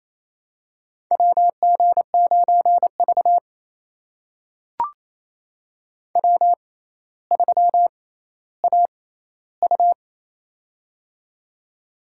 Morse code, heard as a keyed tone, about 28 wpm, spaced as for 14 wpm, 700 Hz. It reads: WG9V W3AU
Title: USA Call Signs - 4 Characters - 6x Character Spacing - 28wpm